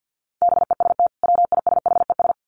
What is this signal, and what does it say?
Morse code, heard as a keyed tone, about 50 wpm, 700 Hz. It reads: BEST WISHES
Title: Common QSO Phrases 50wpm